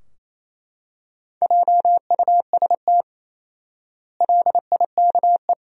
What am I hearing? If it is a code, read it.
Morse code, heard as a keyed tone, about 28 wpm, 700 Hz. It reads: JUST LIKE
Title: 4x Word-Spacing - Sentences from Top 200 Words - 28wpm